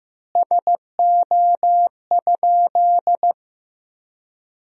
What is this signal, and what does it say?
Morse code, heard as a keyed tone, about 15 wpm, 700 Hz. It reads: SO?
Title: 3x Word-Spacing - Sentences from Top 300 Words - 15wpm